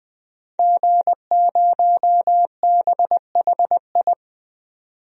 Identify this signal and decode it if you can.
Morse code, heard as a keyed tone, about 21 wpm, 700 Hz. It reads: G0BHI
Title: DXCC-100 Call Signs - 5 Characters - 20wpm